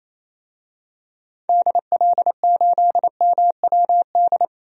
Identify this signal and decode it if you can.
Morse code, heard as a keyed tone, about 28 wpm, 700 Hz. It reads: DL8MWD
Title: DXCC-100 Call Signs - 6 Characters - 28wpm